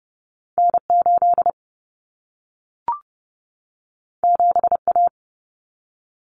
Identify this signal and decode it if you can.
Morse code, heard as a keyed tone, about 30 wpm, 700 Hz. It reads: N8 7A
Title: Common Cut Numbers - 2-Digit 30wpm